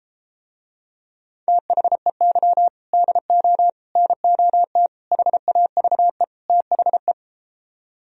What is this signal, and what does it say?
Morse code, heard as a keyed tone, about 33 wpm, 700 Hz. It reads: THEY DO NOT HAVE THE